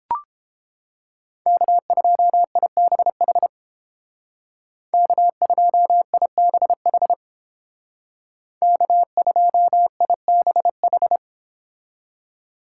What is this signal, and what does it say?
Morse code, heard as a keyed tone, about 30 wpm, 700 Hz. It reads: K2IBH K2IBH K2IBH